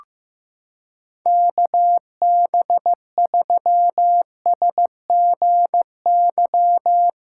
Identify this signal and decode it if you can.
Morse code, heard as a keyed tone, about 15 wpm, 700 Hz. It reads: KB3SGY
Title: USA Callsign DE Callsign - 6 Characters - 15wpm